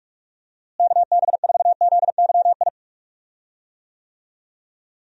Morse code, heard as a keyed tone, about 46 wpm, 700 Hz. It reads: KB4ZYI
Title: USA Call Signs - 6 Characters - Repeated 2x - 45wpm